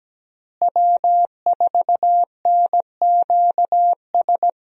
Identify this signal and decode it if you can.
Morse code, heard as a keyed tone, about 17 wpm, 700 Hz. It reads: W4NQS
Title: USA Call Signs - 5 Characters - Repeated 2x - 17wpm